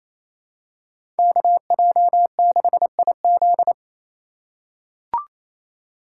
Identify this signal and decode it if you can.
Morse code, heard as a keyed tone, about 27 wpm, 700 Hz. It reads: KJ6IZ
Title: USA Call Signs - 5 Characters - 2x Character Spacing - 28wpm